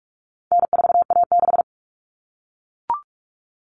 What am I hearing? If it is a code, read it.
Morse code, heard as a keyed tone, about 44 wpm, 700 Hz. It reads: N4A6
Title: Common Cut Numbers - 4-Digit 45wpm